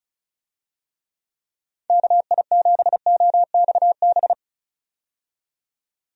Morse code, heard as a keyed tone, about 36 wpm, 700 Hz. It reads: KI7OXB
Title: USA Call Signs - 6 Characters - 35wpm